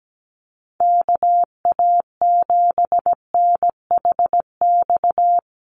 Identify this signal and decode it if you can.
Morse code, heard as a keyed tone, about 17 wpm, 700 Hz. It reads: KA7NHX
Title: USA Call Signs 17wpm